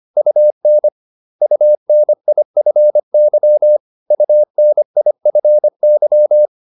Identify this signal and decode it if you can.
Morse code, heard as a keyed tone, about 25 wpm, 600 Hz. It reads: UN UNIFY UNIFY